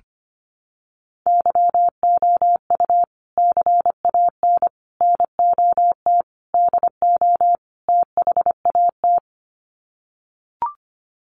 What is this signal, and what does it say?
Morse code, heard as a keyed tone, about 25 wpm, 700 Hz. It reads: YOU CAN NOT DO THAT
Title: Sets of 5 Words 25wpm